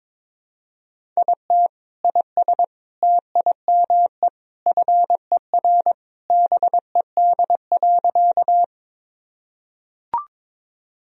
Morse code, heard as a keyed tone, about 21 wpm, 700 Hz. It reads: IT IS TIME FER BED.
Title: Sentences from Top 500 Words and Abbreviations 22wpm